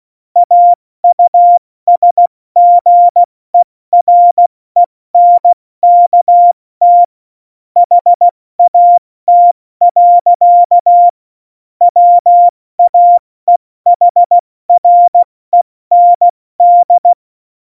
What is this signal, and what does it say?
Morse code, heard as a keyed tone, about 16 wpm, 700 Hz. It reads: AUSGERENKT HAT. WAEHREND
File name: Morsekurs / Witze / Witz 27. Oktober 2020, 70 F80